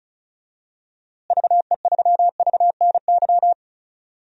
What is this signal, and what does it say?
Morse code, heard as a keyed tone, about 36 wpm, 700 Hz. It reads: VE3VNY
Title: DXCC-100 Call Signs - 6 Characters - Repeated 3x - 35wpm